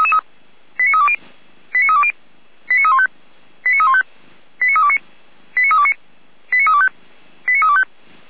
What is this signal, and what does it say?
Five-tone selective call (ZVEI), heard as quick runs of five tones, tones of about 70 ms. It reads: 89320 89320 89315 89315 89329 89329 89325 89325